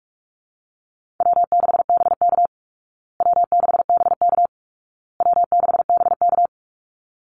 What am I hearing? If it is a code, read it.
Morse code, heard as a keyed tone, about 44 wpm, 700 Hz. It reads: W6BK W6BK W6BK
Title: Easy Call Signs Repeated 3x - 45wpm